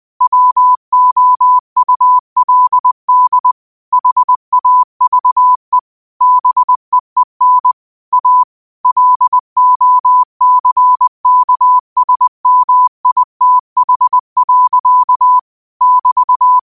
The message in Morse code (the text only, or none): WOULD HAVE BEEN A LOCKSMITH. <BT>